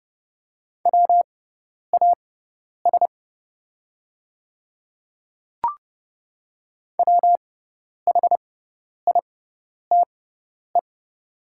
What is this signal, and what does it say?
Morse code, heard as a keyed tone, about 32 wpm, spaced as for 10 wpm, 700 Hz. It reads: WAS WHITE
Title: 6x Character Spacing - Top 300 Words - 30wpm